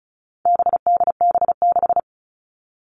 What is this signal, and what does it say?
Morse code, heard as a keyed tone, about 34 wpm, 700 Hz. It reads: BDB6
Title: All Cut Numbers - 4-Digit 35wpm